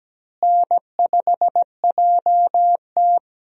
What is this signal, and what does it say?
Morse code, heard as a keyed tone, about 17 wpm, 700 Hz. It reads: N5JT
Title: USA Call Signs - 4 Characters - Repeated 3x - 17wpm